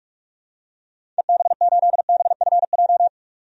Morse code, heard as a keyed tone, about 47 wpm, 700 Hz. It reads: EB8BFJ
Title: DXCC-100 Call Signs - 6 Characters - 45wpm